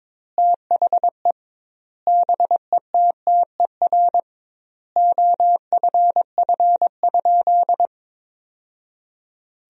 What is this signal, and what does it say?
Morse code, heard as a keyed tone, about 22 wpm, 700 Hz. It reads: THE BETTER OFF?
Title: 2x Word-Spacing - Sentences from Top 500 Words - 22wpm